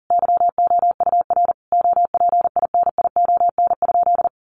Morse code, heard as a keyed tone, about 40 wpm, 700 Hz. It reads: YOUR OPINION?
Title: Sentences from Top 1200 Words 40wpm